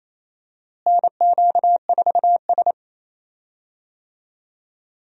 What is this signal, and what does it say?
Morse code, heard as a keyed tone, about 28 wpm, 700 Hz. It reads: NQ4S
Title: USA Call Signs - 4 Characters - Repeated 2x - 28wpm